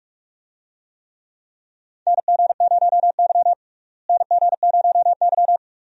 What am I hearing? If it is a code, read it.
Morse code, heard as a keyed tone, about 43 wpm, 700 Hz. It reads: NG0Y NG0Y
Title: USA Call Signs - 4 Characters - Repeated 2x - 45wpm